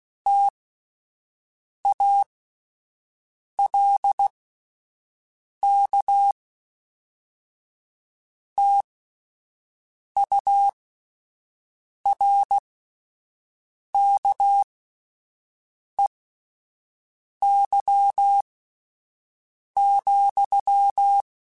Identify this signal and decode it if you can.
Morse code, heard as a keyed tone, about 16 wpm, spaced as for 5 wpm, 800 Hz. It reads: TALK TURKEY,
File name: Misc / Misc-mp3s / Word Pairs @5